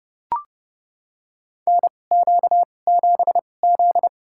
Morse code, heard as a keyed tone, about 30 wpm, spaced as for 22 wpm, 700 Hz. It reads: NQ7Z
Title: USA Call Signs - 4 Characters - 2x Character Spacing - 30wpm